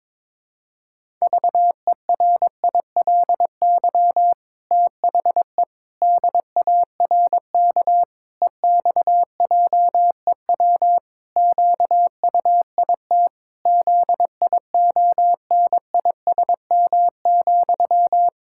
Morse code, heard as a keyed tone, about 21 wpm, 700 Hz. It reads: VERILY THE DARK EXJEW QUIT ZIONISM,
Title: Pangram Challenge v2 - 22wpm